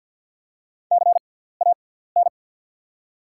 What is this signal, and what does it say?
Morse code, heard as a keyed tone, about 48 wpm, spaced as for 17 wpm, 700 Hz. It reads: CAN
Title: 6x Character Spacing - Top 100 Words - 50wpm